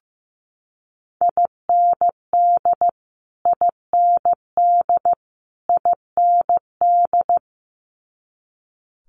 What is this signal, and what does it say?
Morse code, heard as a keyed tone, about 15 wpm, 700 Hz. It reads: IND IND IND
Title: Multi-letter Phonemes Repeated 3x 15wpm